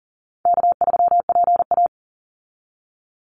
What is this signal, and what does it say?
Morse code, heard as a keyed tone, about 40 wpm, 700 Hz. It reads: K3PA
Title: Easy Call Signs Repeated 3x - 40wpm